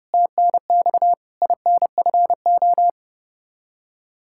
Morse code, heard as a keyed tone, about 30 wpm, 700 Hz. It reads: TNX INFO
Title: Practice QSO - Exchange 3 - More Info - 30wpm